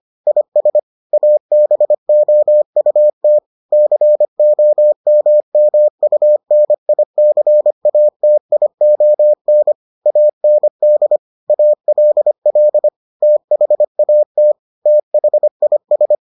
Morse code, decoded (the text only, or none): IS ABOUT COMMUNICATION AND ALL THAT THIS